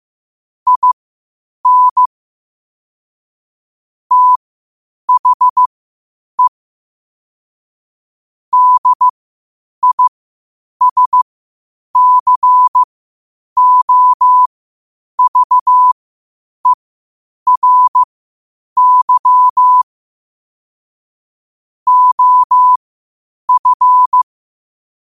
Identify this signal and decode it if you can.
Morse code, heard as a keyed tone, about 15 wpm, spaced as for 8 wpm, 1000 Hz. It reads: IN THE DISCOVERY OF
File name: 8wpm